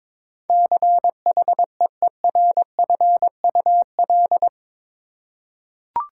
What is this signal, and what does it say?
Morse code, heard as a keyed tone, about 22 wpm, 700 Hz. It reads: CHEERFUL
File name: practice / 22wpm / Eight Letter Words - 22wpm